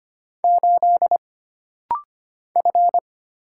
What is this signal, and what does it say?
Morse code, heard as a keyed tone, about 25 wpm, 700 Hz. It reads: 8 F
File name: Single Character (Letter:Number:Punctuation:Prosign) - Rapid-Fire - 25wpm